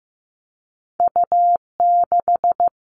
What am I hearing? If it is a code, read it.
Morse code, heard as a keyed tone, about 15 wpm, 700 Hz. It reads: U6
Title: Two Challenging Letter-Number Combinations 15wpm